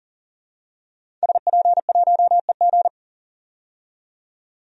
Morse code, heard as a keyed tone, about 40 wpm, 700 Hz. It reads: SP1EG
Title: DXCC-100 Call Signs - 5 Characters - 40wpm